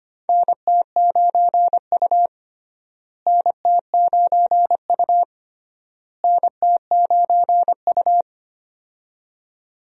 Morse code, heard as a keyed tone, about 25 wpm, 700 Hz. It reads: NT9U NT9U NT9U
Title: USA Call Signs - 4 Characters - Repeated 3x - 25wpm